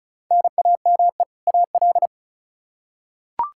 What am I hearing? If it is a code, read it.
Morse code, heard as a keyed tone, about 36 wpm, 700 Hz. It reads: NAME AL
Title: K1USN SST QSO Practice 35wpm